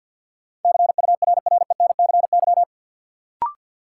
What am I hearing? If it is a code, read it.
Morse code, heard as a keyed tone, about 52 wpm, 700 Hz. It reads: CURRENCY
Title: Sentences from Top 1500 Words - New Word Drill - 50wpm